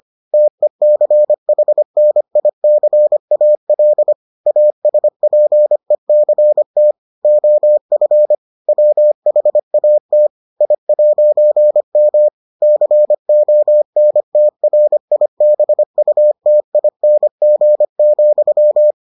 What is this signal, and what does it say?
Morse code, heard as a keyed tone, about 25 wpm, 600 Hz. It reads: TECHNICAL ASPECT OF WHAT I'M CONTRIBUTING,